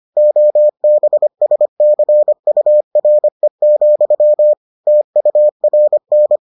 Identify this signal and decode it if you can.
Morse code, heard as a keyed tone, about 25 wpm, 600 Hz. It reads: OBSCURE, TURN